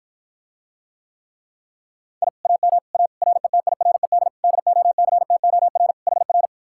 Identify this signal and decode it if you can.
Morse code, heard as a keyed tone, about 54 wpm, 700 Hz. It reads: I AM A RETIRED DOCTOR HR